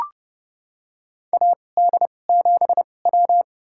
Morse code, heard as a keyed tone, about 30 wpm, spaced as for 22 wpm, 700 Hz. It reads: AD7W
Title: USA Call Signs - 2x Character Spacing - 30wpm